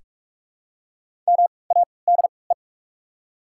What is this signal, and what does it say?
Morse code, heard as a keyed tone, about 44 wpm, spaced as for 25 wpm, 690 Hz. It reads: MADE
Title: 3x Character Spacing - Top 200 Words - 45wpm